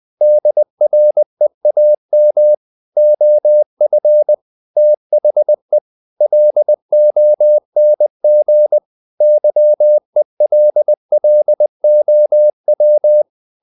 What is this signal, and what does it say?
Morse code, heard as a keyed tone, about 20 wpm, 600 Hz. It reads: DREAM OF THE LONG YELLOW
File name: the old man and the sea part 2 20wpm